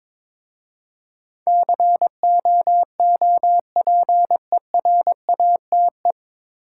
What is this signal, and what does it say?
Morse code, heard as a keyed tone, about 22 wpm, 700 Hz. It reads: COOPERATE
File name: 9 Letter Words - First 3 Letters Given - 22wpm